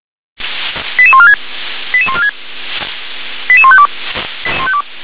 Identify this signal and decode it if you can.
Five-tone selective call (ZVEI), heard as quick runs of five tones, tones of about 70 ms. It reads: 88146 88146 88152